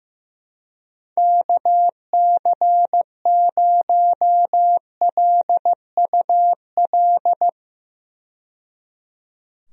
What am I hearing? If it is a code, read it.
Morse code, heard as a keyed tone, about 15 wpm, 700 Hz. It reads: KC0LUL
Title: USA Call Signs - 6 Characters - 15wpm